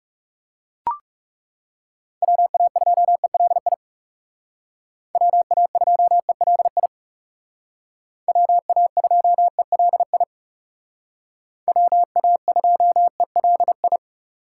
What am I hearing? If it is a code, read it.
Morse code, heard as a keyed tone, about 39 wpm, 700 Hz. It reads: WA2ELI WA2ELI WA2ELI WA2ELI